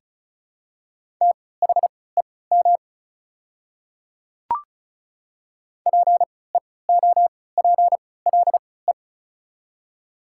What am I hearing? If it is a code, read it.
Morse code, heard as a keyed tone, about 36 wpm, spaced as for 20 wpm, 700 Hz. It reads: THEM PEOPLE
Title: 3x Character Spacing - Top 100 Words - 35wpm